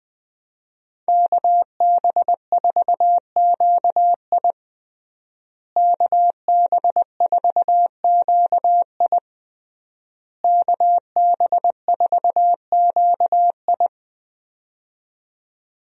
Morse code, heard as a keyed tone, about 21 wpm, 700 Hz. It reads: KB4QI KB4QI KB4QI
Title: USA Call Signs - 5 Characters - Repeated 3x - 20wpm